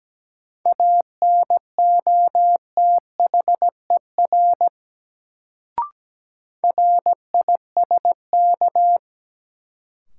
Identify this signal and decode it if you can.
Morse code, heard as a keyed tone, about 17 wpm, 700 Hz. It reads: ANOTHER RISK
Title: Sentences from Top 600 Words - New Word Drill - 17wpm